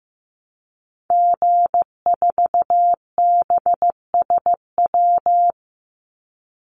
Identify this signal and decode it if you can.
Morse code, heard as a keyed tone, about 15 wpm, 700 Hz. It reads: G4BSW
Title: Call Signs Repeated 3x - 15wpm